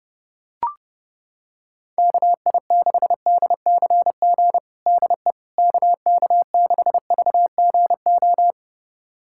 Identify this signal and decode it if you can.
Morse code, heard as a keyed tone, about 31 wpm, 700 Hz. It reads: KI6DCG DE KK6VGO
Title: USA Callsign DE Callsign - 6 Characters - 30wpm